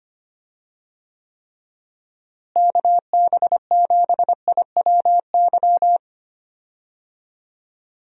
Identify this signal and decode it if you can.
Morse code, heard as a keyed tone, about 25 wpm, 700 Hz. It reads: KB7IWY
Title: USA Call Signs - 6 Characters - Repeated 3x - 25wpm